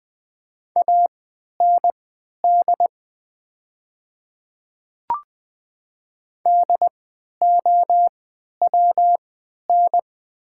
Morse code, heard as a keyed tone, about 20 wpm, spaced as for 11 wpm, 700 Hz. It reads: AND DOWN
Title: 3x Character Spacing - Top 200 Words - 20wpm